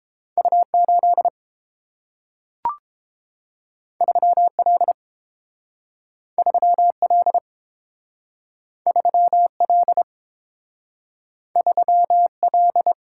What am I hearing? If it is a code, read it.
Morse code, heard as a keyed tone, about 30 wpm, 700 Hz. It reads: U8 3L 3L 3L 3L